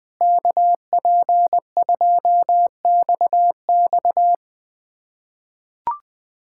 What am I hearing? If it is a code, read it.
Morse code, heard as a keyed tone, about 20 wpm, 700 Hz. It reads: KP2XX
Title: USA Callsign DE Callsign - 5 Characters - 20wpm